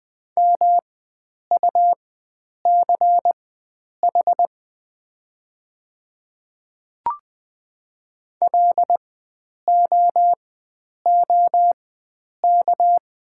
Morse code, heard as a keyed tone, about 20 wpm, spaced as for 9 wpm, 700 Hz. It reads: MUCH LOOK
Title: 4x Character Spacing - Top 300 Words - 20wpm